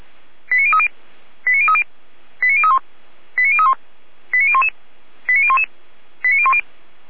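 Five-tone selective call (ZVEI), heard as quick runs of five tones, tones of about 70 ms. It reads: 89030 89030 89031 89031 89011 89011 89010
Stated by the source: ZVEI five-tone selective call received on an SDR.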